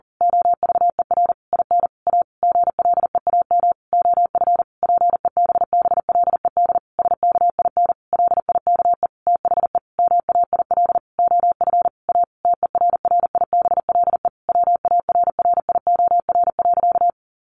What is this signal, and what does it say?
Morse code, heard as a keyed tone, about 40 wpm, 700 Hz. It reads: OVER IN A GLEAM OF PEBBLED SKIN LIKE THE MAIL OF A TERRIBLE WARRIOR.
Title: A Sound Of Thunder 40wpm